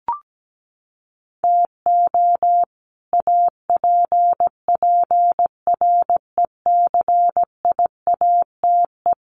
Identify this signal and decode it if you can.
Morse code, heard as a keyed tone, about 17 wpm, 700 Hz. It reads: TO APPRECIATE
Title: Sentences from Top 1300 Words 17wpm